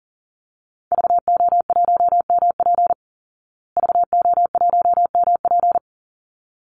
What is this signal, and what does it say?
Morse code, heard as a keyed tone, about 40 wpm, 700 Hz. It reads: VO1MP VO1MP
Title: Call Signs Repeated 3x - 40wpm